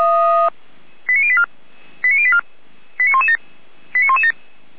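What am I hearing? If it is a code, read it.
Five-tone selective call (ZVEI), heard as quick runs of five tones, tones of about 70 ms. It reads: 89084 89084 89117 89117
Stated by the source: ZVEI five-tone selective call, address 89115, received by SDR.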